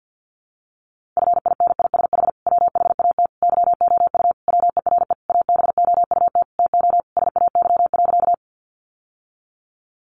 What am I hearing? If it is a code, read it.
Morse code, heard as a keyed tone, about 50 wpm, 700 Hz. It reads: FINISH WHAT YOU WERE ABOUT TO SAY.
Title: Sentences from Top 1200 Words 50wpm